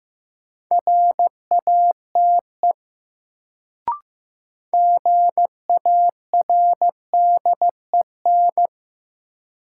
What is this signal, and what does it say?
Morse code, heard as a keyed tone, about 16 wpm, 700 Hz. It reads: RATE GARDEN
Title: Sentences from Top 700 Words - New Word Drill - 15wpm